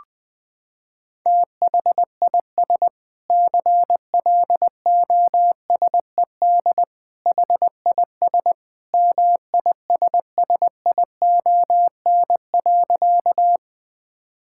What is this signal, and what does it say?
Morse code, heard as a keyed tone, about 21 wpm, 700 Hz. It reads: THIS CLOSED HIS MISSION.
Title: Sentences from Top 1600 Words 20wpm